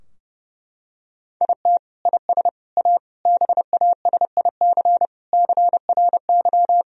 Morse code, heard as a keyed tone, about 30 wpm, 700 Hz. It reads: IT IS A BASIC CRY
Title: Sentences from Top 1000 Words - Review Part 3 - 30wpm